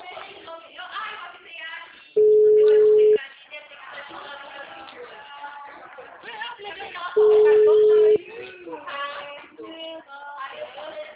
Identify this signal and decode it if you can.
Ringback tone (Europe, 425 Hz), with speech.
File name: out- -2010- - - .51658